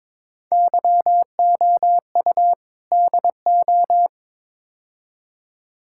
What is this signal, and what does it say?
Morse code, heard as a keyed tone, about 22 wpm, 700 Hz. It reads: YOU DO
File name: practice / 22wpm / Build A Phrase - 5-Word Phrases - 22wpm